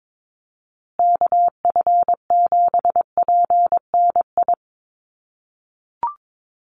Morse code, heard as a keyed tone, about 22 wpm, 700 Hz. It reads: KF7PNI
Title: USA Call Signs Repeated 3x 22wpm